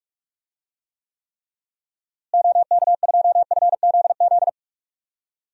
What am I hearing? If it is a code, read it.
Morse code, heard as a keyed tone, about 41 wpm, 700 Hz. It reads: OK2FZZ
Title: DXCC-100 Call Signs - 6 Characters - Repeated 3x - 45wpm